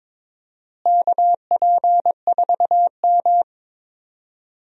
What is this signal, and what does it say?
Morse code, heard as a keyed tone, about 22 wpm, 700 Hz. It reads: KP4M
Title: DXCC-100 Call Signs - 4 Characters - Repeated 2x - 22wpm